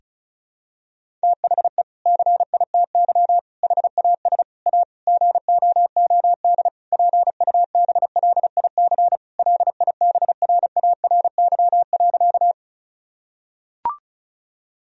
Morse code, heard as a keyed tone, about 34 wpm, 700 Hz. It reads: THE CITY HAS A GOOD PUBLIC LIBRARY.
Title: Sentences from Top 1000 Words - Review Part 4 - 35wpm